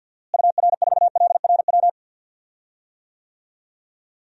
Morse code, heard as a keyed tone, about 46 wpm, 700 Hz. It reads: UR4LRW
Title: DXCC-100 Call Signs - 6 Characters - Repeated 3x - 50wpm